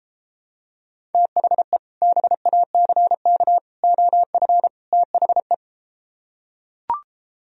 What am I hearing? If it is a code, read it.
Morse code, heard as a keyed tone, about 33 wpm, 700 Hz. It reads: THE BACK OF THE